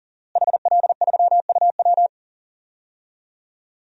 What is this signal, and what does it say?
Morse code, heard as a keyed tone, about 40 wpm, 700 Hz. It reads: HL3UW
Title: DXCC-100 Call Signs - 5 Characters - Repeated 2x - 40wpm